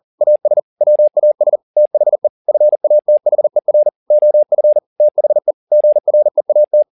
Morse code, heard as a keyed tone, about 40 wpm, 600 Hz. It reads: AS WAS THE FATHER OF THE GREAT
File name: the old man and the sea 2 40wpm